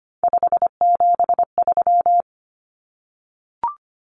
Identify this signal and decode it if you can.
Morse code, heard as a keyed tone, about 25 wpm, 700 Hz. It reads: 573
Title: Numbers - 3-Digit 25wpm